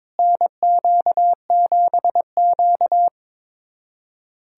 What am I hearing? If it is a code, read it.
Morse code, heard as a keyed tone, about 22 wpm, 700 Hz. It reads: NQ7Q
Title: USA Callsign DE Callsign - 4 Characters - 22wpm